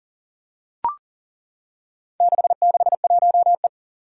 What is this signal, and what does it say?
Morse code, heard as a keyed tone, about 40 wpm, 700 Hz. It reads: 661E
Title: Four Challenging Letter-Number Combinations 40wpm